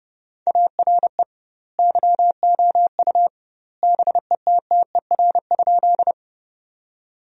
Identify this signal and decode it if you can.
Morse code, heard as a keyed tone, about 32 wpm, 700 Hz. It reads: ARE YOU BETTER?
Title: 2x Word-Spacing - Sentences from Top 400 Words - 30wpm